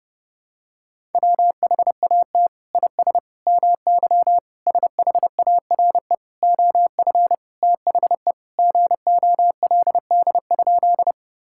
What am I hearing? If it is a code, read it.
Morse code, heard as a keyed tone, about 32 wpm, 700 Hz. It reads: WHAT IS MY SHARE OF THE GOLD?
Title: Sentences from Top 1000 Words - Review Part 6 - 30wpm